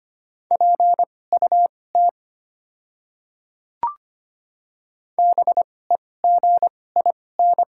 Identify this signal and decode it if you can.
Morse code, heard as a keyed tone, about 25 wpm, spaced as for 18 wpm, 700 Hz. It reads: PUT BEGIN
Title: 2x Character Spacing - Top 300 Words - 25wpm